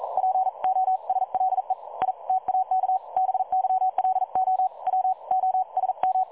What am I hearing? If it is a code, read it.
Morse code, heard as a keyed tone, about 42 wpm, 750 Hz. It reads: POSLE I TAK BYLO W OSM
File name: dlazaawansowanych / nagraniazpasma / websdr recording 3565.0kHz